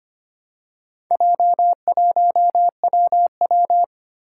Morse code, heard as a keyed tone, about 25 wpm, 700 Hz. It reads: J1WW
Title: Four Challenging Letter-Number Combinations 25wpm